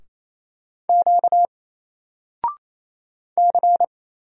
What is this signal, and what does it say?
Morse code, heard as a keyed tone, about 27 wpm, 700 Hz. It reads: Q C
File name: Single-Letter-Number-28wpm